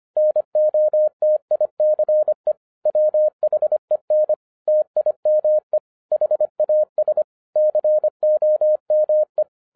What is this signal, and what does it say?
Morse code, heard as a keyed tone, about 25 wpm, 610 Hz. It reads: NOTICE WHEN TIME HAS COME